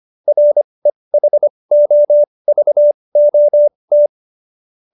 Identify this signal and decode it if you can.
Morse code, heard as a keyed tone, about 25 wpm, spaced as for 20 wpm, 590 Hz. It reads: REHOVOT